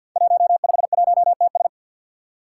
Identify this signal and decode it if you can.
Morse code, heard as a keyed tone, about 50 wpm, 700 Hz. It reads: 151TS
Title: Five Challenging Letter-Number Combinations 50wpm